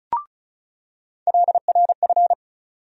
Morse code, heard as a keyed tone, about 34 wpm, 700 Hz. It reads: LRF
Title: USA Call Sign Postfixes - 3 Letters 35wpm